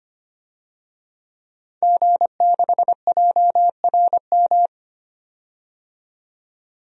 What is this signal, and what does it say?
Morse code, heard as a keyed tone, about 25 wpm, 700 Hz. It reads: G6JRM
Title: DXCC-100 Call Signs - 5 Characters - 25wpm